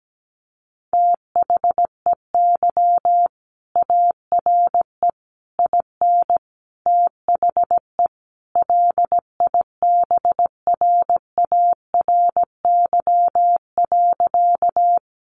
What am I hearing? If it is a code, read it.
Morse code, heard as a keyed tone, about 17 wpm, 700 Hz. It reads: THEY ARE IN THE LIBRARY.
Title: Sentences from Top 600 Words 17wpm